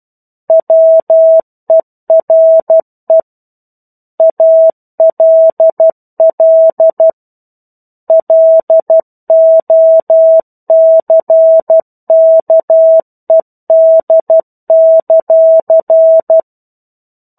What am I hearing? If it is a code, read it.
Morse code, heard as a keyed tone, about 12 wpm, 650 Hz. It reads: WERE ALL LOCKED;